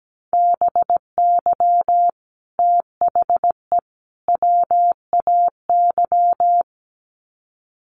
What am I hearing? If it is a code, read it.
Morse code, heard as a keyed tone, about 17 wpm, 700 Hz. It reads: BY THE WAY
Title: Sets of 3 Words 17wpm